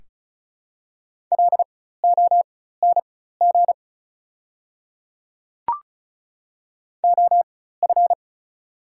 Morse code, heard as a keyed tone, about 36 wpm, spaced as for 16 wpm, 700 Hz. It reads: LONG OF
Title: 4x Character Spacing - Top 100 Words - 35wpm